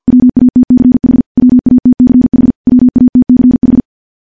Morse code, heard as a keyed tone, about 50 wpm, 250 Hz. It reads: WATCH WATCH WATCH
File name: morse / morse-training-50WPM